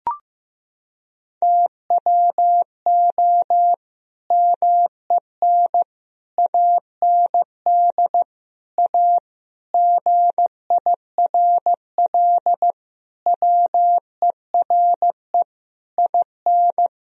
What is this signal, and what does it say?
Morse code, heard as a keyed tone, about 15 wpm, 700 Hz. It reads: TWO MEN AND A GIRL WERE IN
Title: Sentences from Top 1000 Words - Review Part 3 - 15wpm